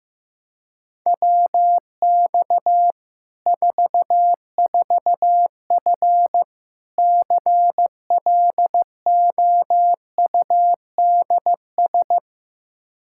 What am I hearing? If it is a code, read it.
Morse code, heard as a keyed tone, about 16 wpm, 700 Hz. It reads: WX 44F CLOUDS
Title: K1USN SST QSO Practice 15wpm